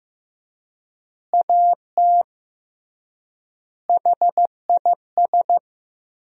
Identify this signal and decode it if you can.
Morse code, heard as a keyed tone, about 16 wpm, 700 Hz. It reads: AT HIS
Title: 3x Word-Spacing - Sentences from Top 300 Words - 15wpm